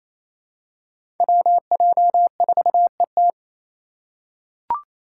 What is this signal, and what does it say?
Morse code, heard as a keyed tone, about 28 wpm, 700 Hz. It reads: WJ4ET
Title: USA Call Signs - 2x Character Spacing - 28wpm